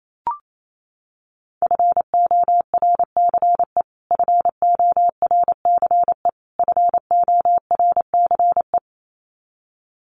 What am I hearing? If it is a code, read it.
Morse code, heard as a keyed tone, about 28 wpm, 700 Hz. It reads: FORCE FORCE FORCE
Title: Top 500 Words Repeated 28wpm